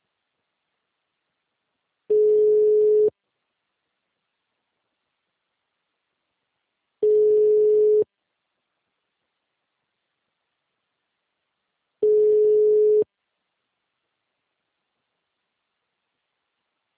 Ringback tone (Europe, 425 Hz).
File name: out- -2030- - - .30033